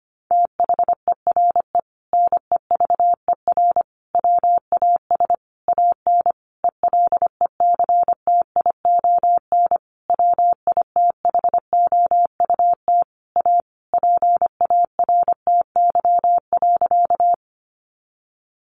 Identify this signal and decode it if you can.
Morse code, heard as a keyed tone, about 25 wpm, 700 Hz. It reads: THERE NEVER WAS AN ELECTION WITHOUT A PARTY.
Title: Sentences from Top 1500 Words 25wpm